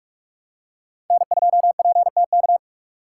Morse code, heard as a keyed tone, about 41 wpm, 700 Hz. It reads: N2PTK